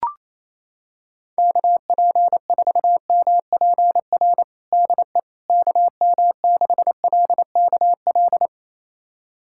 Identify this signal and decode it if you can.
Morse code, heard as a keyed tone, about 28 wpm, 700 Hz. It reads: KP4MPR DE KM6LKL
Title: USA Callsign DE Callsign - 6 Characters - 28wpm